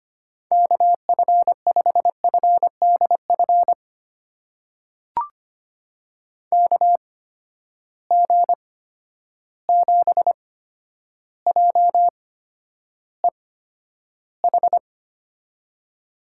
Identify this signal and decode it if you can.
Morse code, heard as a keyed tone, about 25 wpm, spaced as for 11 wpm, 700 Hz. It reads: KF5FDF KG7JEH
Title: USA Call Signs - 8x Character Spacing - 25wpm